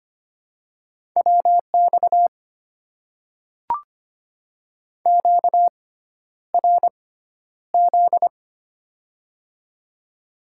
Morse code, heard as a keyed tone, about 25 wpm, spaced as for 11 wpm, 700 Hz. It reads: WX QRZ
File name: Top 90 QSO Elements - 6x Character Spacing - 25wpm